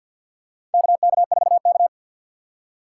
Morse code, heard as a keyed tone, about 47 wpm, 700 Hz. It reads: KK4K
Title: USA Call Signs - 4 Characters - 6x Character Spacing - 50wpm